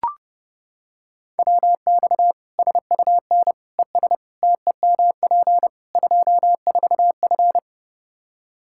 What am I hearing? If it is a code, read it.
Morse code, heard as a keyed tone, about 30 wpm, 700 Hz. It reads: WX SUN ES TEMP 24F
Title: Practice QSO - Exchange 2 - Info - 30wpm